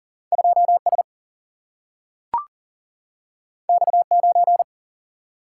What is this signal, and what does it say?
Morse code, heard as a keyed tone, about 39 wpm, 700 Hz. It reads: 2S X9
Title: Single-Letter Single-Number 40wpm